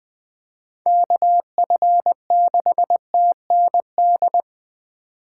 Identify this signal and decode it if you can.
Morse code, heard as a keyed tone, about 21 wpm, 700 Hz. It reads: KF6TND
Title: USA Call Signs - 6 Characters - 4x Character Spacing - 20wpm